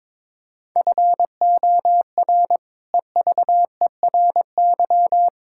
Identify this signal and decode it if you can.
Morse code, heard as a keyed tone, about 22 wpm, 700 Hz. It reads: FOR EVERY
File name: practice / 22wpm / Sentences with 8-Word Phrases - 22wpm